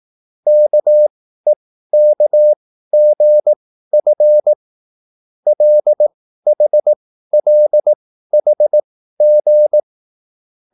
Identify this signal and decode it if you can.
Morse code, heard as a keyed tone, about 18 wpm, spaced as for 13 wpm, 600 Hz. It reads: KEKGF LHLHG
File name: QSL / CW lesson